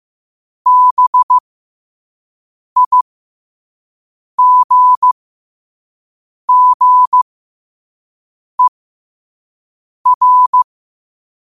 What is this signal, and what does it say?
Morse code, heard as a keyed tone, about 15 wpm, spaced as for 5 wpm, 1000 Hz. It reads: BIGGER
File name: qotdfiles / 5wpm